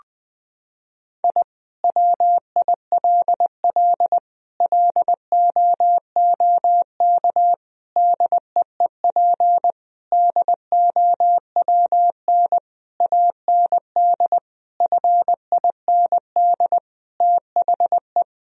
Morse code, heard as a keyed tone, about 20 wpm, 700 Hz. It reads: I WILL LOOK DEEP DOWN AND FIND THE
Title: Sentences from Top 1000 Words - Review Part 3 - 20wpm